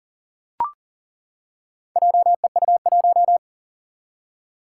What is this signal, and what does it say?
Morse code, heard as a keyed tone, about 40 wpm, 700 Hz. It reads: JEU1
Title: Four Challenging Letter-Number Combinations 40wpm